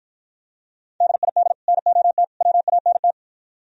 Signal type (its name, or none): Morse code